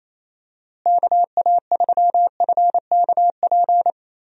Morse code, heard as a keyed tone, about 28 wpm, 700 Hz. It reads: KA3FKP
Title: USA Call Signs - 6 Characters - Repeated 3x - 28wpm